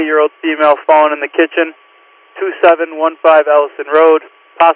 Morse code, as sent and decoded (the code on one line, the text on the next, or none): none